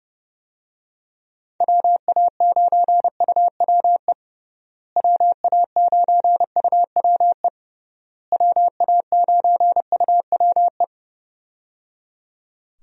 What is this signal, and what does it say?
Morse code, heard as a keyed tone, about 30 wpm, 700 Hz. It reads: WA9UWE WA9UWE WA9UWE
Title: USA Call Signs - 6 Characters - Repeated 3x - 30wpm